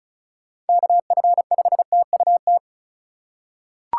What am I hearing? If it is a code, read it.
Morse code, heard as a keyed tone, about 36 wpm, 700 Hz. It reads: KF5TUT
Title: USA Call Signs - 6 Characters - 2x Character Spacing - 35wpm